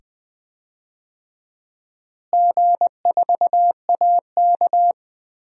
Morse code, heard as a keyed tone, about 20 wpm, 700 Hz. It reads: G4AK
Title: DXCC-100 Call Signs - 4 Characters - Repeated 2x - 20wpm